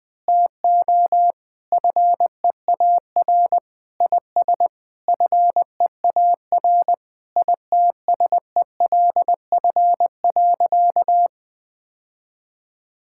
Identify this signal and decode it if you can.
Morse code, heard as a keyed tone, about 21 wpm, 700 Hz. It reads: TO FEAR IS FEAR ITSELF.
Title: Sentences with 8-Word Phrases - 20wpm